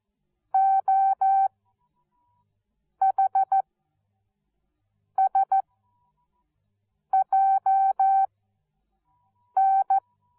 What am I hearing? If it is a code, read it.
Morse code, heard as a keyed tone, about 14 wpm, spaced as for 5 wpm, 770 Hz. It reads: OHSJN